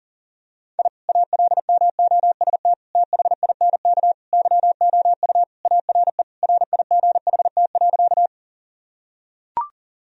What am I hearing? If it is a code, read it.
Morse code, heard as a keyed tone, about 40 wpm, 700 Hz. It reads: I ALMOST THINK YOU ARE RIGHT.
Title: Sentences from Top 1000 Words - Review Part 4 - 40wpm